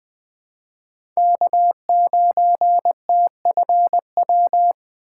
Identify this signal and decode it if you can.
Morse code, heard as a keyed tone, about 20 wpm, 700 Hz. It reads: K9TFW
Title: USA Call Signs - 5 Characters - 2x Character Spacing - 20wpm